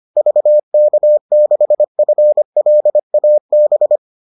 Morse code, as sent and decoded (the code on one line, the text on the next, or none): ...- -.- -.... ..-. .-.. .- -...
VK6FLAB